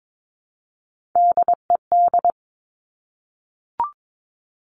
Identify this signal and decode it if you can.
Morse code, heard as a keyed tone, about 22 wpm, 700 Hz. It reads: DED
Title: Top Three Letter Combinations in Words 22wpm